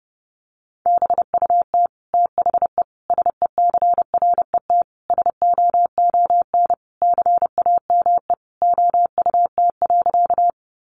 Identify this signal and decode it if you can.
Morse code, heard as a keyed tone, about 30 wpm, 700 Hz. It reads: BUT THE SECRET SOON CAME OUT.
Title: Sentences from Top 1200 Words 30wpm